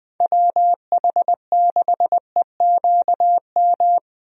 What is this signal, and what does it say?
Morse code, heard as a keyed tone, about 20 wpm, 700 Hz. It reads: WH6EQM
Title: USA Call Signs - 6 Characters - Repeated 2x - 20wpm